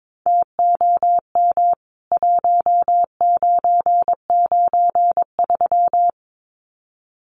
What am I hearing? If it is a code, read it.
Morse code, heard as a keyed tone, about 22 wpm, 700 Hz. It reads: TOM 1993
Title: CWT Exchange 22wpm